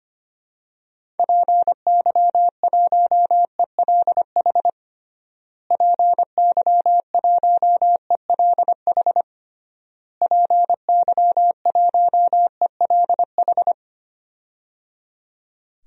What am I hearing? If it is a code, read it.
Morse code, heard as a keyed tone, about 25 wpm, 700 Hz. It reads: PY1ELH PY1ELH PY1ELH
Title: DXCC-100 Call Signs - 6 Characters - Repeated 3x - 25wpm